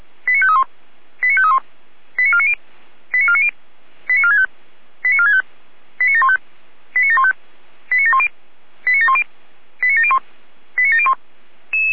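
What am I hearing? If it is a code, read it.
Five-tone selective call (ZVEI), heard as quick runs of five tones, tones of about 70 ms. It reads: 89531 89531 89490 89490 89465 89465 89715 89715 89710 89710 89701 89701
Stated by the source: ZVEI five-tone selective call received on SDR.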